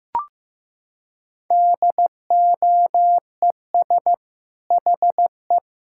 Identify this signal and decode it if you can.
Morse code, heard as a keyed tone, about 16 wpm, 690 Hz. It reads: DOES HE
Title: Sentences from Top 1000 Words - Review Part 5 - 15wpm